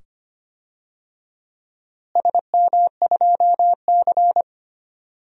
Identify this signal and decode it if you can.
Morse code, heard as a keyed tone, about 25 wpm, 700 Hz. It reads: SM2C
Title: DXCC-100 Call Signs - 4 Characters - Repeated 2x - 25wpm